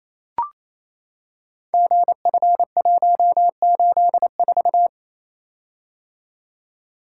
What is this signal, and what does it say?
Morse code, heard as a keyed tone, about 27 wpm, 700 Hz. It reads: GF184
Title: Code Groups - 5 Letter:Numbers - 28wpm